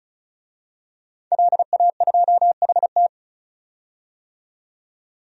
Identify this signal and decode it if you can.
Morse code, heard as a keyed tone, about 34 wpm, 700 Hz. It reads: LA2HT
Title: DXCC-100 Call Signs - 5 Characters - 35wpm